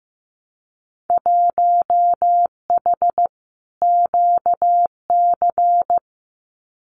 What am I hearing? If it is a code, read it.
Morse code, heard as a keyed tone, about 15 wpm, 700 Hz. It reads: 1H QC
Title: Winter Field Day Exchanges 15wpm